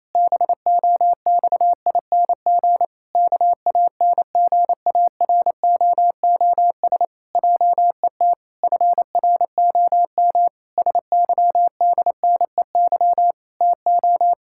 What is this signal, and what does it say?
Morse code, heard as a keyed tone, about 27 wpm, 700 Hz. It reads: BOXING KANGAROOS JET FROM SYDNEY TO
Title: Pangram Challenge v2 - 28wpm